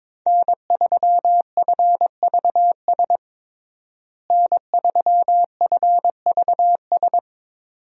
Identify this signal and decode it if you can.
Morse code, heard as a keyed tone, about 21 wpm, 700 Hz. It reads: N3FVS N3FVS
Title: USA Call Signs - 5 Characters - Repeated 2x - 22wpm